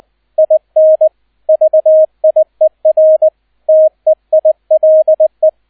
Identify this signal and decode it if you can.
Morse code, heard as a keyed tone, about 20 wpm, 630 Hz, with radio noise. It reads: IN VIER TEILE
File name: Erster Streich 20wpm